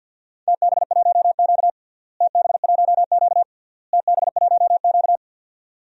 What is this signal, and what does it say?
Morse code, heard as a keyed tone, about 50 wpm, 700 Hz. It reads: TB1Q TB1Q TB1Q
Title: DXCC-100 Call Signs - 4 Characters - Repeated 3x - 50wpm